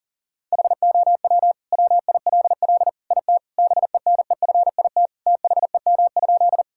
Morse code, heard as a keyed tone, about 40 wpm, 690 Hz. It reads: HOW WILL IT BENEFIT THEM?
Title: Sentences from Top 1000 Words - Review Part 6 - 40wpm